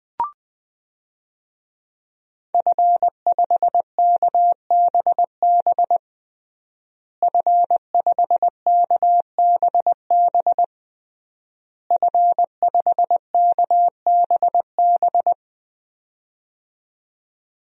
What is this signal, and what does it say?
Morse code, heard as a keyed tone, about 20 wpm, 700 Hz. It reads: F5KBB F5KBB F5KBB
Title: DXCC-100 Call Signs - 5 Characters - Repeated 3x - 20wpm